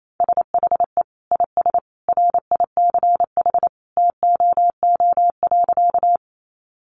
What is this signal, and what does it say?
Morse code, heard as a keyed tone, about 28 wpm, 700 Hz. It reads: SHE IS RICH TOO.
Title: Sentences from Top 1100 Words 28wpm